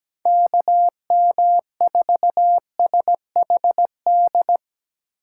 Morse code, heard as a keyed tone, about 17 wpm, 700 Hz. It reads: KM4SHD
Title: USA Call Signs - 6 Characters - 2x Character Spacing - 17wpm